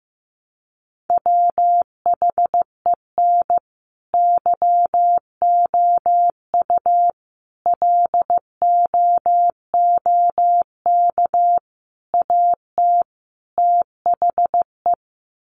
Morse code, heard as a keyed tone, about 15 wpm, 700 Hz. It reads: WHEN YOU LOOK AT THE
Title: Sets of 5 Words 15wpm